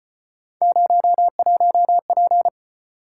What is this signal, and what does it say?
Morse code, heard as a keyed tone, about 35 wpm, 700 Hz. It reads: 01P